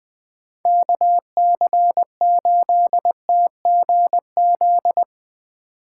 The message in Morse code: KC8TGZ

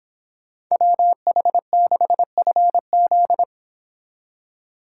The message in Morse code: WH6FZ

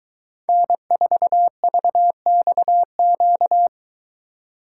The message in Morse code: N4VXQ

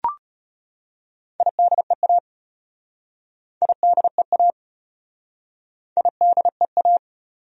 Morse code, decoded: IDEA IDEA IDEA

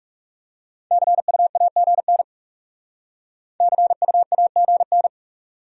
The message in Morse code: CUAGN CUAGN

